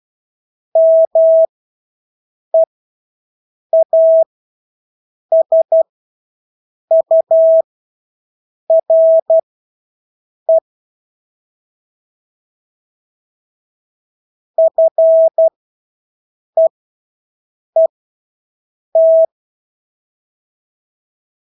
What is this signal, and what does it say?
Morse code, heard as a keyed tone, about 12 wpm, spaced as for 6 wpm, 650 Hz. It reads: MEASURE FEET
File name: Engelske ord 12-6 wpm